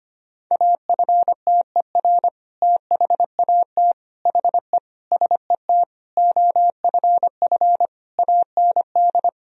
Morse code, heard as a keyed tone, about 25 wpm, 700 Hz. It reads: AFTER THAT HE SET OFF AND
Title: Sentences from Top 1000 Words - Review Part 1 - 25wpm